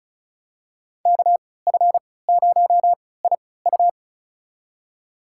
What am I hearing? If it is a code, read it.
Morse code, heard as a keyed tone, about 36 wpm, spaced as for 20 wpm, 700 Hz. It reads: KF0IU
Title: USA Call Signs - 5 Characters - 3x Character Spacing - 35wpm